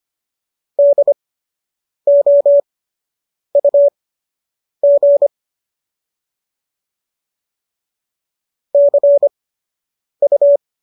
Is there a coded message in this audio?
Yes: Morse code from 0.8 to 10.6 s, heard as a keyed tone, about 25 wpm, spaced as for 8 wpm, 570 Hz. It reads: DOUG CU